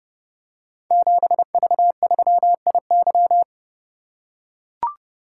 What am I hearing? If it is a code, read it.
Morse code, heard as a keyed tone, about 30 wpm, 700 Hz. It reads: 7V3IY
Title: Code Groups - 5 Letter:Numbers - 30wpm